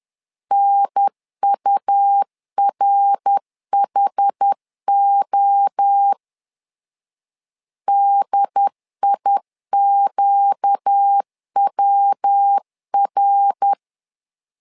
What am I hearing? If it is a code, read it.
Morse code, heard as a keyed tone, about 10 wpm, 800 Hz. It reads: NURHO DIQWR
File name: LEZIONI / 4 Q 10wpm 2210